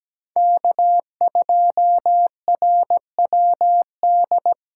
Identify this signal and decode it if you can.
Morse code, heard as a keyed tone, about 20 wpm, 700 Hz. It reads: K2RWD
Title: Speed Racing - USA 1x3 Call Signs 17wpm